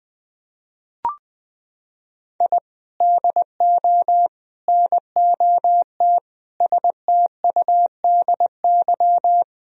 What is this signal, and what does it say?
Morse code, heard as a keyed tone, about 21 wpm, 700 Hz. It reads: I DO NOT STUDY
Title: Sentences from Top 1000 Words - Review Part 1 - 20wpm